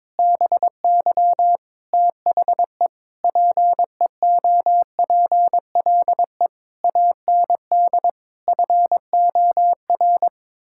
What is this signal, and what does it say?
Morse code, heard as a keyed tone, about 22 wpm, 700 Hz. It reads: BY THE PEOPLE AND FOR
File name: practice / 22wpm / Sets of 8 Words - 22wpm